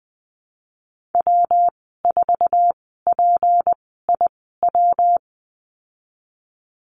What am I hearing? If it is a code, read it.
Morse code, heard as a keyed tone, about 20 wpm, spaced as for 15 wpm, 700 Hz. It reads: W4PIW
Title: 167 Lesson 27g - Number 7 - Call Signs 20wpm